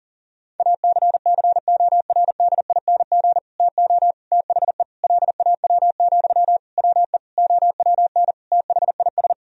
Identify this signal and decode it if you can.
Morse code, heard as a keyed tone, about 40 wpm, 700 Hz. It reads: ACCORDING TO THE LAW, WE OWN THIS
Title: Sentences from Top 1000 Words - Review Part 4 - 40wpm